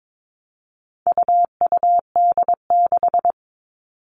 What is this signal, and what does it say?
Morse code, heard as a keyed tone, about 22 wpm, 700 Hz. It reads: UUD6
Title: All Cut Numbers - 4-Digit 22wpm